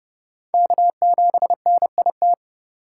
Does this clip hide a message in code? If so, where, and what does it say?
Yes: Morse code from 0.5 to 2.3 s, heard as a keyed tone, about 30 wpm, 700 Hz. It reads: K7NIT